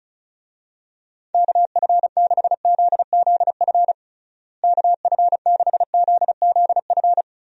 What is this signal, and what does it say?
Morse code, heard as a keyed tone, about 36 wpm, 700 Hz. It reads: KF6ZZF KF6ZZF
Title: USA Call Signs - 6 Characters - Repeated 3x - 35wpm